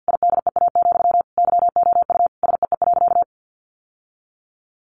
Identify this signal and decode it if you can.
Morse code, heard as a keyed tone, about 50 wpm, 700 Hz. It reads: IDEA, YOU SEE.
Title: Sentences from Top 600 Words 50wpm